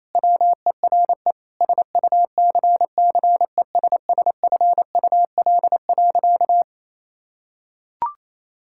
Morse code, heard as a keyed tone, about 27 wpm, 700 Hz. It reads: WERE SUCCESSFUL.
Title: Sentences from Top 1000 Words - Review Part 6 - 28wpm